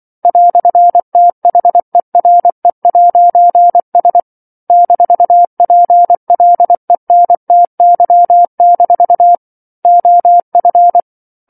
Morse code, heard as a keyed tone, about 24 wpm, 700 Hz. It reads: "THERE'S -PLENTY- OF
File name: 07-Chapter-07 24 WPM